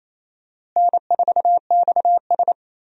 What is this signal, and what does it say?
Morse code, heard as a keyed tone, about 28 wpm, 700 Hz. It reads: N4XS
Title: USA Call Signs - 4 Characters - Repeated 2x - 28wpm